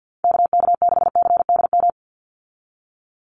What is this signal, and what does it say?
Morse code, heard as a keyed tone, about 50 wpm, 700 Hz. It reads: KK6CDM
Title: USA Call Signs 50wpm